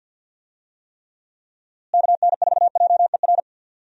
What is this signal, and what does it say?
Morse code, heard as a keyed tone, about 51 wpm, 700 Hz. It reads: KN4JER
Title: USA Call Signs - 6 Characters - Repeated 2x - 50wpm